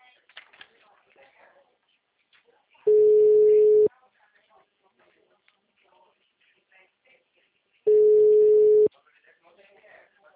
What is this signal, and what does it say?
Ringback tone (Europe, 425 Hz), with speech.
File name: out- -2009- - - .10189